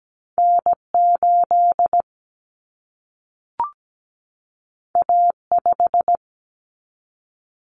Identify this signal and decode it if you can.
Morse code, heard as a keyed tone, about 17 wpm, 700 Hz. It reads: N8 A5
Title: Common Cut Numbers - 2-Digit 17wpm